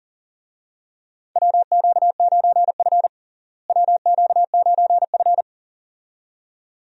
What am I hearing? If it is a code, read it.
Morse code, heard as a keyed tone, about 39 wpm, 700 Hz. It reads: WQ9F WQ9F
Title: USA Call Signs - 4 Characters - Repeated 2x - 40wpm